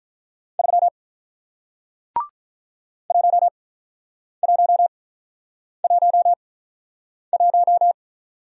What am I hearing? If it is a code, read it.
Morse code, heard as a keyed tone, about 48 wpm, spaced as for 9 wpm, 700 Hz. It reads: 3 1111